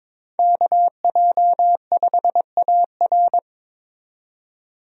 Morse code, heard as a keyed tone, about 22 wpm, 700 Hz. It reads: KJ5AR
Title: USA Call Signs - 5 Characters - 22wpm